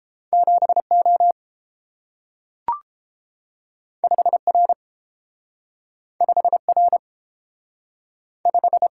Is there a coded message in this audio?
Yes: Morse code from 0.3 to 8.9 s, heard as a keyed tone, about 30 wpm, 700 Hz. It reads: 7O 5R 5R 5